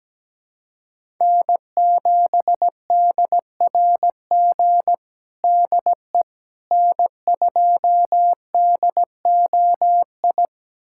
Morse code, heard as a keyed tone, about 17 wpm, 700 Hz. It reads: N7DRG DE N2DOI
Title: USA Callsign DE Callsign - 5 Characters - 17wpm